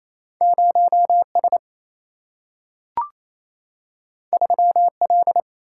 Morse code, heard as a keyed tone, about 28 wpm, 700 Hz. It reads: 0 S 3 L
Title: Single-Letter Single-Number 28wpm